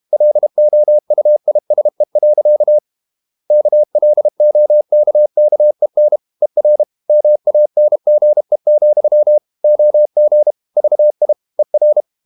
Morse code, heard as a keyed tone, about 32 wpm, 600 Hz. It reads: LOUISE. KLOKKEN ER MANGE, OG VI ER
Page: Ferie 32 wpm